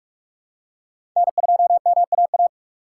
Morse code, heard as a keyed tone, about 46 wpm, 700 Hz. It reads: N2MAA
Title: USA Call Signs - 5 Characters - 4x Character Spacing - 45wpm